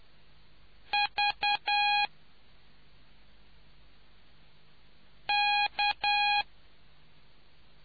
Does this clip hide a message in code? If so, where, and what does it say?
Yes: Morse code from 0.9 to 6.4 s, heard as a keyed tone, about 10 wpm, 820 Hz. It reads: V K